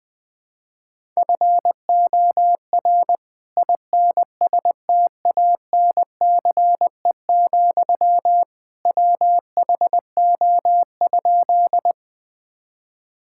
Morse code, heard as a keyed tone, about 20 wpm, 700 Hz. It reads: FOR INSTANCE, WHO?
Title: Sentences from Top 1000 Words - Review Part 5 - 20wpm